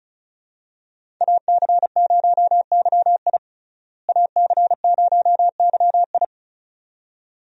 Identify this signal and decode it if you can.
Morse code, heard as a keyed tone, about 36 wpm, 700 Hz. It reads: AC0YI AC0YI
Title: USA Call Signs - 5 Characters - Repeated 2x - 35wpm